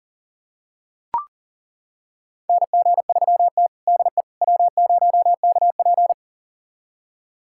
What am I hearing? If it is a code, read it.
Morse code, heard as a keyed tone, about 40 wpm, 700 Hz. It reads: NG3T DE W0KP
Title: USA Callsign DE Callsign - 4 Characters - 40wpm